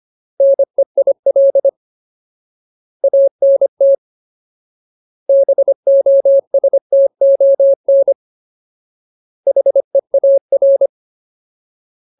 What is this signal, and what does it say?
Morse code, heard as a keyed tone, about 25 wpm, 550 Hz. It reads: NEIL ANT BOSTON HEAR
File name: Ultimate Word List / 25WPM Ultimate